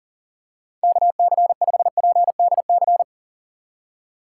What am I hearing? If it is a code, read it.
Morse code, heard as a keyed tone, about 40 wpm, 700 Hz. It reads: KC5PDC
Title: USA Call Signs - 2x Character Spacing - 40wpm